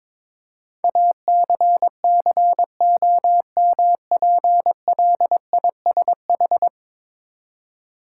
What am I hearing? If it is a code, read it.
Morse code, heard as a keyed tone, about 22 wpm, 700 Hz. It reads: ACCOMPLISH
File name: practice / 22wpm / 10 Letter Words - First 3 Letters Given - 22wpm